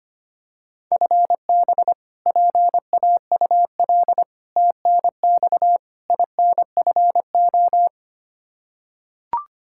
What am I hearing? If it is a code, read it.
Morse code, heard as a keyed tone, about 25 wpm, 700 Hz. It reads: FB PAUL TNX INFO
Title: Practice QSO - Exchange 3 - More Info - 25wpm